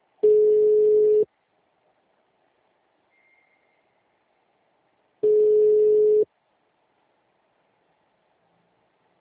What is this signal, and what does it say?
Ringback tone (Europe, 425 Hz).
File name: out- -2010- - - .12